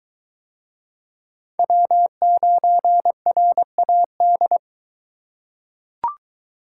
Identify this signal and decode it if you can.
Morse code, heard as a keyed tone, about 22 wpm, 700 Hz. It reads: W9RAD